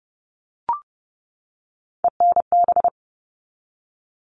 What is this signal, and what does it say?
Morse code, heard as a keyed tone, about 30 wpm, 700 Hz. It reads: ENB
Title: All Cut Numbers - 3-Digit 30wpm